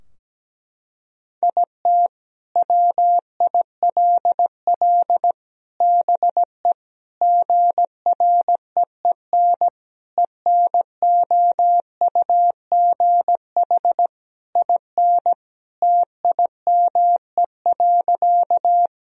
Morse code, heard as a keyed tone, about 17 wpm, 700 Hz. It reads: IT WILL BE GREEN ENOUGH IN TIME.
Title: Sentences from Top 1000 Words - Review Part 2 - 17wpm